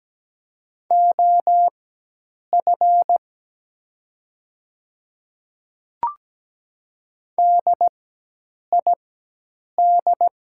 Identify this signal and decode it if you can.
Morse code, heard as a keyed tone, about 17 wpm, spaced as for 8 wpm, 700 Hz. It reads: OF DID
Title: 4x Character Spacing - Top 100 Words - 17wpm